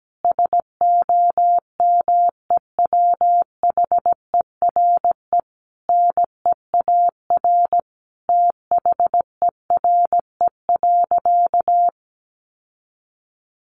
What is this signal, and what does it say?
Morse code, heard as a keyed tone, about 17 wpm, 700 Hz. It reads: SOMEWHERE NEAR THERE.
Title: Sentences from Top 1400 Words 17wpm